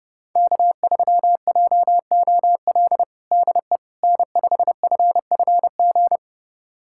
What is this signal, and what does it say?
Morse code, heard as a keyed tone, about 32 wpm, 700 Hz. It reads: K3JOL DE N5FFG
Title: USA Callsign DE Callsign - 5 Characters - 30wpm